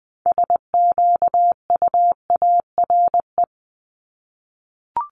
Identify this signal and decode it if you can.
Morse code, heard as a keyed tone, about 20 wpm, 700 Hz. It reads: SQUARE
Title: Top 751-1000 Words 20wpm